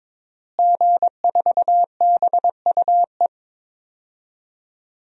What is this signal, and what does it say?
Morse code, heard as a keyed tone, about 22 wpm, 700 Hz. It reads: G4BUE
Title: DXCC-100 Call Signs - 5 Characters - Repeated 3x - 22wpm